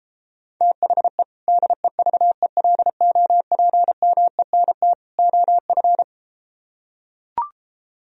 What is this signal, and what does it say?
Morse code, heard as a keyed tone, about 33 wpm, 700 Hz. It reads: THE DEVELOPMENT OF